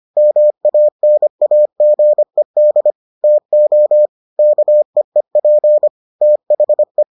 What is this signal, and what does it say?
Morse code, heard as a keyed tone, about 25 wpm, 600 Hz. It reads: MANAGED TO KEEP THE